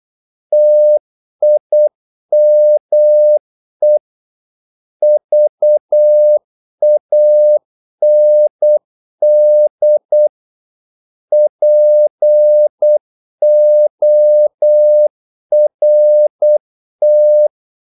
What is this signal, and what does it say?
Morse code, heard as a keyed tone, about 8 wpm, 600 Hz. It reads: TIME VAND PORT
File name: Korte ord DK 8wpm